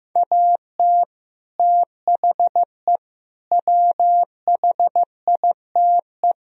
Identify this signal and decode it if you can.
Morse code, heard as a keyed tone, about 15 wpm, 700 Hz. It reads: AT THE WHITE
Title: Speed-Racing - Sets of 4 Words 15wpm